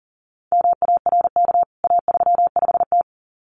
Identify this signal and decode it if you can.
Morse code, heard as a keyed tone, about 38 wpm, 700 Hz. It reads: MARK A35T
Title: CWT Exchange 40wpm